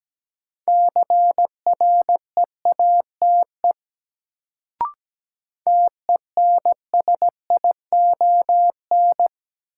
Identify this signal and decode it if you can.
Morse code, heard as a keyed tone, about 17 wpm, 700 Hz. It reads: CREATE TENSION
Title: Sentences from Top 1600 Words - New Word Drill Alt - 17wpm